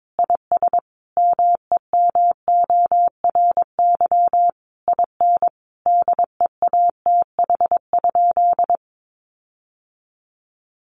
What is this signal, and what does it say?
Morse code, heard as a keyed tone, about 22 wpm, 700 Hz. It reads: IS MEMORY IN DEATH?
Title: Sentences from Top 1300 Words 22wpm